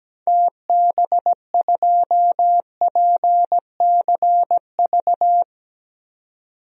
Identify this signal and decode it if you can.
Morse code, heard as a keyed tone, about 17 wpm, 700 Hz. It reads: TB2PCV
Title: DXCC-100 Call Signs - 6 Characters - 17wpm